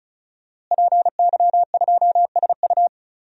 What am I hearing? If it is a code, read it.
Morse code, heard as a keyed tone, about 36 wpm, 700 Hz. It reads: PY2SU
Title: DXCC-100 Call Signs - 5 Characters - Repeated 2x - 35wpm